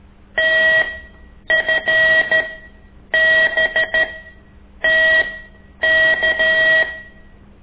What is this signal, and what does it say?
Morse code, heard as a keyed tone, about 13 wpm, spaced as for 8 wpm, 1900 Hz. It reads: TFBTK
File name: ik1weg2 / CORSI CW